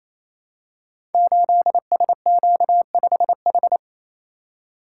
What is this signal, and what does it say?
Morse code, heard as a keyed tone, about 29 wpm, 700 Hz. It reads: 8SQ5H
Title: Code Groups - 5 Letter:Numbers - 28wpm